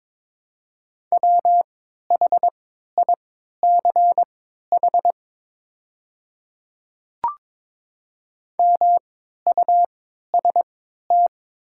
Morse code, heard as a keyed tone, about 22 wpm, spaced as for 12 wpm, 700 Hz. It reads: WHICH MUST
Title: 3x Character Spacing - Top 300 Words - 22wpm